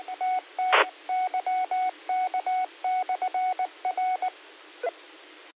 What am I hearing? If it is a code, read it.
Morse code, heard as a keyed tone, about 19 wpm, 740 Hz. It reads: TT YK/R